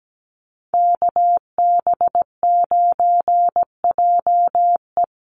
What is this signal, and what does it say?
Morse code, heard as a keyed tone, about 17 wpm, 700 Hz. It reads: KB9JE
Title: USA Call Signs 17wpm